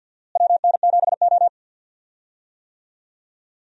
Morse code, heard as a keyed tone, about 53 wpm, 700 Hz. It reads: WN7O
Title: USA Call Signs - 4 Characters - Repeated 3x - 50wpm